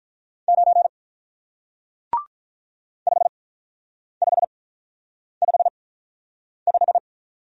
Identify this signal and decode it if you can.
Morse code, heard as a keyed tone, about 45 wpm, spaced as for 9 wpm, 700 Hz. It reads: 9 5555